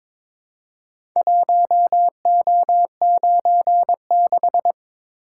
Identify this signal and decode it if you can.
Morse code, heard as a keyed tone, about 22 wpm, 700 Hz. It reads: 1O96
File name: practice / 22wpm / Four Letter-Number Combinations - 22wpm